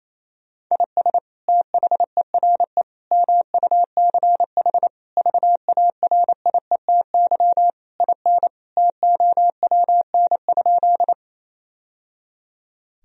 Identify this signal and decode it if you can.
Morse code, heard as a keyed tone, about 28 wpm, 700 Hz. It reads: IS THERE MUCH VARIETY IN TOWN?
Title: Sentences from Top 1000 Words - Review Part 4 - 28wpm